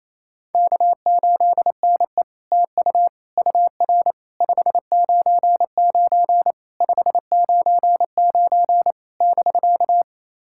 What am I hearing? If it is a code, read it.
Morse code, heard as a keyed tone, about 28 wpm, 700 Hz. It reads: K8NE TU UR 599 599 <BK>
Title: SOTA QSO Practice 28wpm